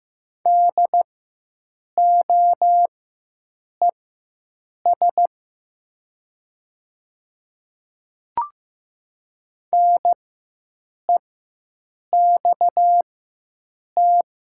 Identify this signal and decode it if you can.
Morse code, heard as a keyed tone, about 15 wpm, spaced as for 7 wpm, 700 Hz. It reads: DOES NEXT
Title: 4x Character Spacing - Top 300 Words - 15wpm